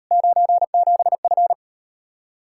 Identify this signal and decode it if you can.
Morse code, heard as a keyed tone, about 39 wpm, 700 Hz. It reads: 97F